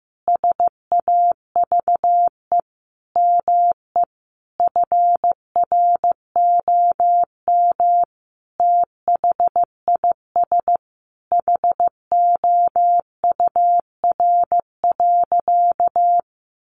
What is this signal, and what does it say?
Morse code, heard as a keyed tone, about 15 wpm, 700 Hz. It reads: SAVE ME FROM THIS HOUR.
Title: Sentences from Top 1100 Words 15wpm